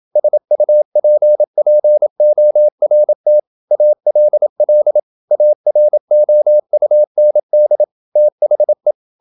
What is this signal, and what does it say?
Morse code, heard as a keyed tone, about 27 wpm, 600 Hz. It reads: SUPPORT ALL AROUND THE
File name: Never 0027wpm